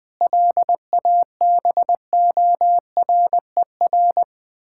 Morse code, heard as a keyed tone, about 20 wpm, 700 Hz. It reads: LABORER
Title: K1USN SST QSO Practice 20wpm